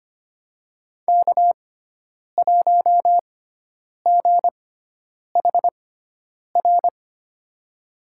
Morse code, heard as a keyed tone, about 25 wpm, spaced as for 9 wpm, 700 Hz. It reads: K1GHR
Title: USA Call Signs - 5 Characters - 6x Character Spacing - 25wpm